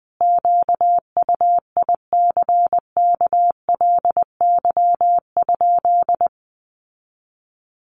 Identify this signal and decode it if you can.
Morse code, heard as a keyed tone, about 20 wpm, 700 Hz. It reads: QUICKLY?
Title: Sentences from Top 700 Words 20wpm